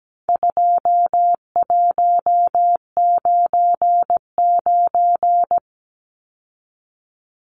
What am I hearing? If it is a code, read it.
Morse code, heard as a keyed tone, about 17 wpm, 700 Hz. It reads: 2199
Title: Numbers - 4-Digit 17wpm